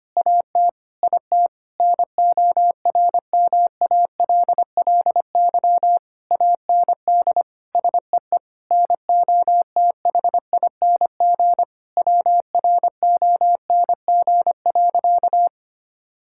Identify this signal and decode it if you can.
Morse code, heard as a keyed tone, about 25 wpm, 700 Hz. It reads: AT IT NORMALLY AND SEE NOTHING WRONG.
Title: Sentences from Top 1000 Words - Review Part 6 - 25wpm